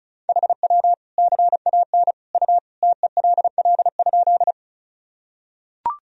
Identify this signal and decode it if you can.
Morse code, heard as a keyed tone, about 34 wpm, 700 Hz. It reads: HW CAN U TELL?
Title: Sentences from Top 200 Words and Abbreviations 35wpm